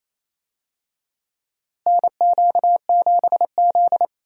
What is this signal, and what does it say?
Morse code, heard as a keyed tone, about 28 wpm, 700 Hz. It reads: NQ7Z
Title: USA Call Signs - 4 Characters - Repeated 3x - 28wpm